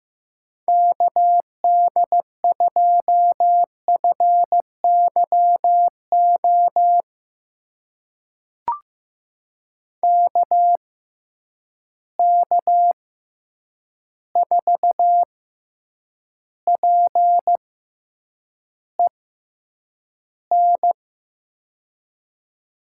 Morse code, heard as a keyed tone, about 15 wpm, spaced as for 8 wpm, 700 Hz. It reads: KD2FYO KK4PEN
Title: USA Call Signs - 6 Characters - 6x Character Spacing - 15wpm